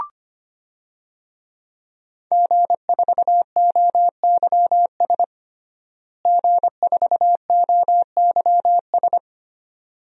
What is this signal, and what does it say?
Morse code, heard as a keyed tone, about 25 wpm, 700 Hz. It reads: G4OYS G4OYS
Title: DXCC-100 Call Signs - 5 Characters - Repeated 2x - 25wpm